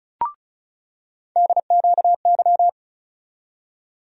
Morse code, heard as a keyed tone, about 34 wpm, 700 Hz. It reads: DQY
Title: USA Call Sign Postfixes - 3 Letters 35wpm